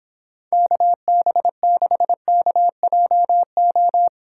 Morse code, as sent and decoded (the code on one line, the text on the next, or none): -.- -... -.... -.- .--- ---
KB6KJO